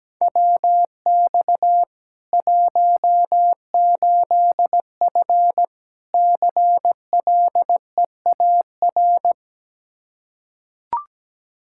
Morse code, heard as a keyed tone, about 17 wpm, 700 Hz. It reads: WX 18F CLEAR
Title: Practice QSO - Exchange 2 - Info - 17wpm